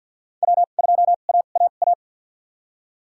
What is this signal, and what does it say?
Morse code, heard as a keyed tone, about 47 wpm, spaced as for 35 wpm, 700 Hz. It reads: W2AAA
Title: USA Call Signs - 2x Character Spacing - 50wpm